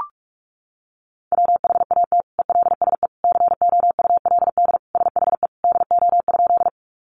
Morse code, heard as a keyed tone, about 44 wpm, 700 Hz. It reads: WHAT ELSE COULD SHE DO?
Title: Sentences from Top 800 Words 45wpm